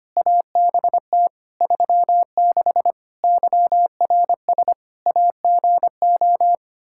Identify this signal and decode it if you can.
Morse code, heard as a keyed tone, about 25 wpm, 700 Hz. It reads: ABT 36 YRS AGO
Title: Practice QSO - Exchange 3 - More Info - 25wpm